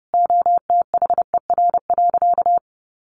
Morse code, heard as a keyed tone, about 30 wpm, 700 Hz. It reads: OTHER.
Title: Sentences from Top 1400 Words 30wpm